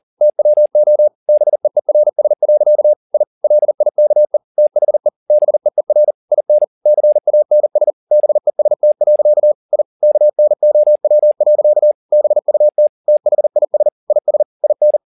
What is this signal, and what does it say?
Morse code, heard as a keyed tone, about 40 wpm, 600 Hz. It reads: TWO BEERS. I LIKE THE BEER IN CANS BEST. I KNOW. BUT THIS IS IN